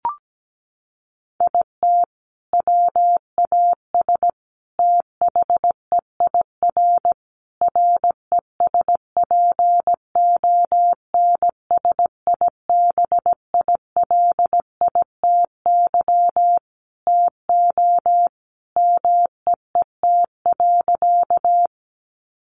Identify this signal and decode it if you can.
Morse code, heard as a keyed tone, about 17 wpm, 700 Hz. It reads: IT WAS THEIR RESPONSIBILITY TO MEET.
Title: Sentences from Top 1200 Words 17wpm